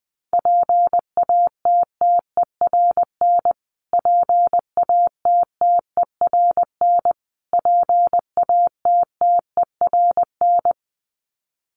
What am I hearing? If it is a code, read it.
Morse code, heard as a keyed tone, about 20 wpm, 700 Hz. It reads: PATTERN PATTERN PATTERN
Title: Top 500 Words Repeated 20wpm